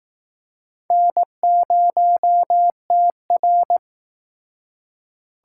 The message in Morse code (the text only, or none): N0TR